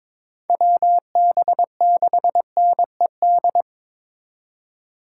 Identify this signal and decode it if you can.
Morse code, heard as a keyed tone, about 26 wpm, 700 Hz. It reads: WB6NED
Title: Speed Racing - USA 2x3 Call Signs 22wpm